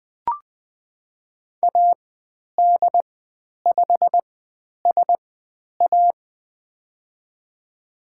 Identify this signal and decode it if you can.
Morse code, heard as a keyed tone, about 20 wpm, spaced as for 10 wpm, 700 Hz. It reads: AD5SA
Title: 102 Lesson 17g - Letter u - Call Signs 20wpm